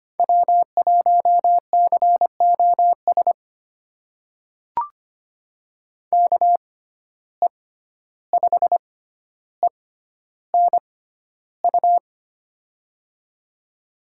Morse code, heard as a keyed tone, about 25 wpm, spaced as for 12 wpm, 700 Hz. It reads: W1COS KE5ENU
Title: USA Call Signs - 6x Character Spacing - 25wpm